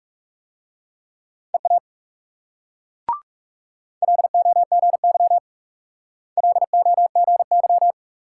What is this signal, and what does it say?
Morse code, heard as a keyed tone, about 37 wpm, 700 Hz. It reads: EA LOGY LOGY
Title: Speed-Racing Multi-letter Phonemes - 30wpm